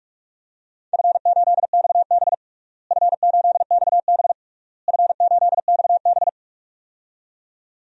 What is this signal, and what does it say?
Morse code, heard as a keyed tone, about 46 wpm, 700 Hz. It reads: F8XB F8XB F8XB
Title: DXCC-100 Call Signs - 4 Characters - Repeated 3x - 45wpm